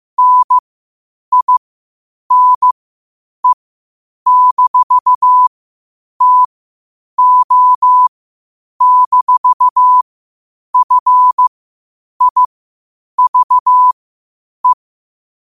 Morse code, heard as a keyed tone, about 15 wpm, spaced as for 8 wpm, 1000 Hz. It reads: NINE-TO-FIVE
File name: qotdfiles / 8wpm